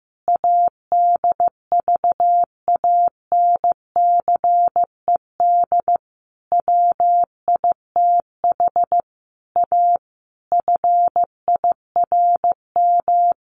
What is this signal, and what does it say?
Morse code, heard as a keyed tone, about 14 wpm, 700 Hz. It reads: ADVANCED WITH A FIRM
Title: Sentences from Top 1400 Words 15wpm